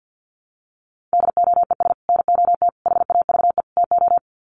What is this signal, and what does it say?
Morse code, heard as a keyed tone, about 48 wpm, 700 Hz. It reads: DOES NOT HAVE TO
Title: Sets of 4 Words 50wpm